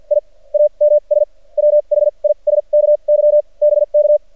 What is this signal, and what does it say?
Morse code, heard as a keyed tone, about 48 wpm, spaced as for 33 wpm, 600 Hz. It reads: IAMS WHISKY BK
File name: archived / CRSnet